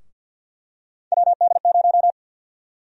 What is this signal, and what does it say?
Morse code, heard as a keyed tone, about 46 wpm, 700 Hz. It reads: WD0
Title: USA Call Sign Prefixes - Repeated 2x - 50wpm